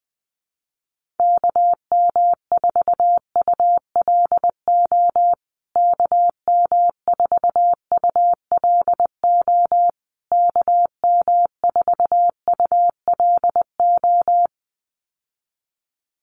Morse code, heard as a keyed tone, about 19 wpm, 700 Hz. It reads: KM4ULO KM4ULO KM4ULO
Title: USA Call Signs Repeated 3x 20wpm